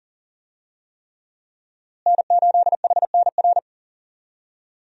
Morse code, heard as a keyed tone, about 40 wpm, 700 Hz. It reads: N8HNR
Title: USA Call Signs - 5 Characters - Repeated 3x - 40wpm